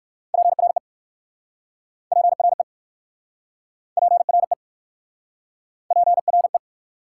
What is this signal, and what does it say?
Morse code, heard as a keyed tone, about 60 wpm, 700 Hz. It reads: PRE PRE PRE PRE